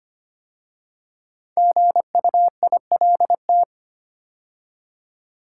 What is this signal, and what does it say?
Morse code, heard as a keyed tone, about 25 wpm, 700 Hz. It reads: GUILT
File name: 5 Letter Words - First 3 Letters Given - 25wpm